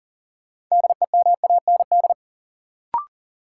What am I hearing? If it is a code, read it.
Morse code, heard as a keyed tone, about 40 wpm, 700 Hz. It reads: DEMAND
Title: Sentences from Top 800 Words - New Word Drill - 40wpm